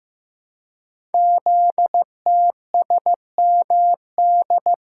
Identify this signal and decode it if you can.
Morse code, heard as a keyed tone, about 15 wpm, 700 Hz. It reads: ZTSMD
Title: Code Groups - 5 Letters - 15wpm